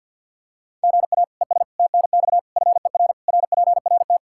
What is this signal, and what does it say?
Morse code, heard as a keyed tone, about 51 wpm, 700 Hz. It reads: GA ES TNX FER RPRT
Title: Practice QSO - Exchange 1 - Beginning - 50wpm